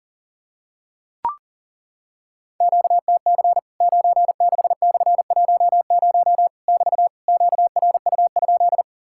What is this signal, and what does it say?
Morse code, heard as a keyed tone, about 40 wpm, 700 Hz. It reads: QTC 96/10 <BT> QRU?
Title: Worked All Europe DX Contest - QTC Exchange - 40wpm